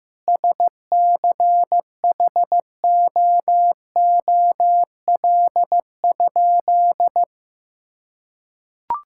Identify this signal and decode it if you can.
Morse code, heard as a keyed tone, about 15 wpm, 700 Hz. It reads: SCHOOL?
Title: Sentences from Top 1000 Words - Review Part 5 - 15wpm